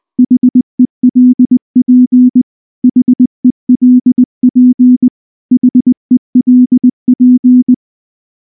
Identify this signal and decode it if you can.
Morse code, heard as a keyed tone, about 19 wpm, 250 Hz. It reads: HELP HELP HELP
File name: morse / morse-training-20WPM-5